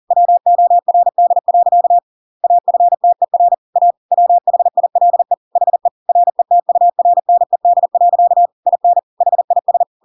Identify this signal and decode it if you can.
Morse code, heard as a keyed tone, about 40 wpm, 700 Hz. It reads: WORD. AFTER A WHILE HE RETURNED. IN HIS
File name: Pinocchio / Pinocchio ch8 40 40 ik1hge